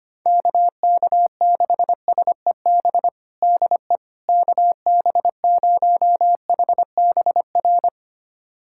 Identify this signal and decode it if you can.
Morse code, heard as a keyed tone, about 25 wpm, 700 Hz. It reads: KK6SEB DE KB0HBR
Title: USA Callsign DE Callsign - 6 Characters - 25wpm